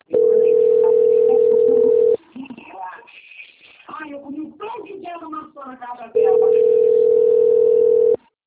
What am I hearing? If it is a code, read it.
Ringback tone (North America), 440 + 480 Hz, with speech.
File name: out- - - - - .24184